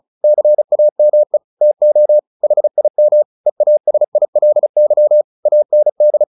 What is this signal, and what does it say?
Morse code, heard as a keyed tone, about 34 wpm, 600 Hz. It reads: CAME TO HIM EASILY AND
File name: the old man and the sea / the old man and the sea 1 35wpm